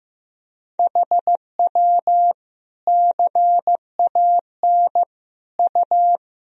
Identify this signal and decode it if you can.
Morse code, heard as a keyed tone, about 16 wpm, 700 Hz. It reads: HW CAN U
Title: Sentences from Top 100 Words and Abbreviations 15wpm